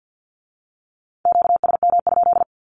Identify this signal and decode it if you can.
Morse code, heard as a keyed tone, about 50 wpm, 700 Hz. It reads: QSM?
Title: Q-Codes with Spoken Meaning - 50wpm